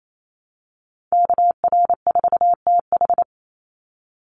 Morse code, heard as a keyed tone, about 28 wpm, 700 Hz. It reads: KR4TH
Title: Call Signs - 28wpm